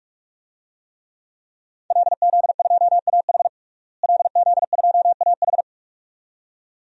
Morse code, heard as a keyed tone, about 46 wpm, 700 Hz. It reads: LZ2AH LZ2AH
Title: DXCC-100 Call Signs - 5 Characters - Repeated 2x - 45wpm